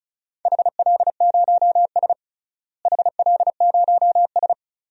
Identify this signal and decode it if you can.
Morse code, heard as a keyed tone, about 36 wpm, 700 Hz. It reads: HL0S HL0S
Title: DXCC-100 Call Signs - 4 Characters - Repeated 2x - 35wpm